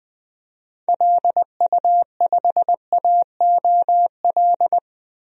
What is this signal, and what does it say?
Morse code, heard as a keyed tone, about 20 wpm, 700 Hz. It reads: LU5AOL
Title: DXCC-100 Call Signs - 6 Characters - 20wpm